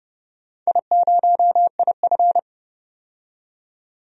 Morse code, heard as a keyed tone, about 31 wpm, 700 Hz. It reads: I0IF
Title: DXCC-100 Call Signs - 4 Characters - 30wpm